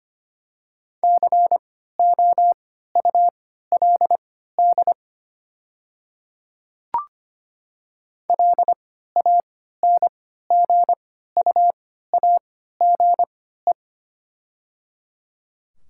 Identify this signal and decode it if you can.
Morse code, heard as a keyed tone, about 25 wpm, spaced as for 14 wpm, 700 Hz. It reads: COULD LANGUAGE
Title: 3x Character Spacing - Top 500 Words - 25wpm